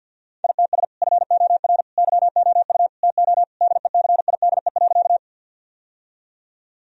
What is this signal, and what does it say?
Morse code, heard as a keyed tone, about 51 wpm, 700 Hz. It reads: ITS FOR YOU TO DECIDE.
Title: Sentences from Top 1000 Words - Review Part 3 - 50wpm